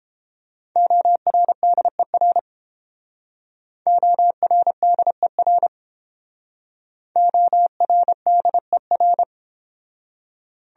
Morse code, heard as a keyed tone, about 30 wpm, 700 Hz. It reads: ORDER ORDER ORDER